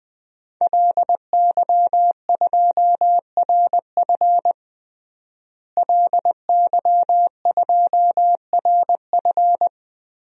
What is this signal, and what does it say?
Morse code, heard as a keyed tone, about 21 wpm, 700 Hz. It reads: LY2RF LY2RF
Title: DXCC-100 Call Signs - 5 Characters - Repeated 2x - 20wpm